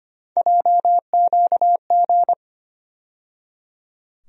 Morse code, heard as a keyed tone, about 25 wpm, 700 Hz. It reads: JQG
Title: USA Call Sign Postfixes - 3 Letters 25wpm